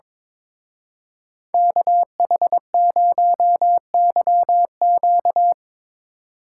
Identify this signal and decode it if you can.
Morse code, heard as a keyed tone, about 24 wpm, 700 Hz. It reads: KH0YQ
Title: Speed Racing - USA 2x2 Call Signs 22wpm